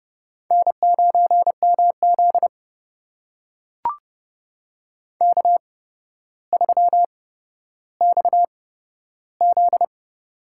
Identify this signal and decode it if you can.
Morse code, heard as a keyed tone, about 30 wpm, spaced as for 13 wpm, 700 Hz. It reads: N9MZ K3XZ
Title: USA Call Signs - 4 Characters - 8x Character Spacing - 30wpm